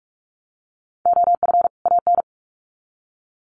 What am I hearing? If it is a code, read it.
Morse code, heard as a keyed tone, about 44 wpm, 700 Hz. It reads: OF AN
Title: Sets of 2 Words 45wpm